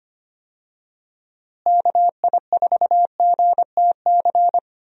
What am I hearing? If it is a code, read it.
Morse code, heard as a keyed tone, about 25 wpm, 700 Hz. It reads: KI4GTC
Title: USA Call Signs - 6 Characters - Repeated 2x - 25wpm